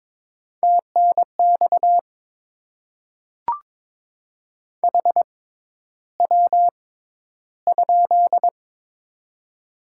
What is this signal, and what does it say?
Morse code, heard as a keyed tone, about 21 wpm, spaced as for 11 wpm, 700 Hz. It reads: TNX HW?
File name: Top 20 QSO Elements - 6x Character Spacing - 22wpm